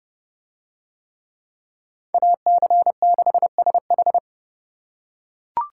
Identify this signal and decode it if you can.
Morse code, heard as a keyed tone, about 29 wpm, 700 Hz. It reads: AC6SH